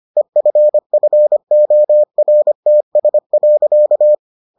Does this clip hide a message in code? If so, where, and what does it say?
Yes: Morse code from 0.2 to 4.1 s, heard as a keyed tone, about 25 wpm, 600 Hz. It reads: EFFORTS.